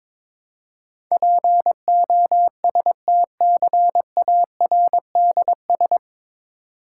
Morse code, heard as a keyed tone, about 22 wpm, 690 Hz. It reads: POSTCARDS
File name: Difficult Words with Hints - 22wpm